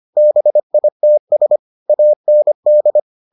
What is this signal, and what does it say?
Morse code, heard as a keyed tone, about 25 wpm, 600 Hz. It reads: BITS AND